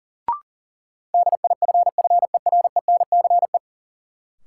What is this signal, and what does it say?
Morse code, heard as a keyed tone, about 40 wpm, 700 Hz. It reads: DIFFERENCE
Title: Sentences from Top 800 Words - New Word Drill - 40wpm